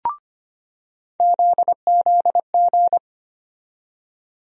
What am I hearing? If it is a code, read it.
Morse code, heard as a keyed tone, about 25 wpm, 700 Hz. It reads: ZZG
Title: USA Call Sign Postfixes - 3 Letters 25wpm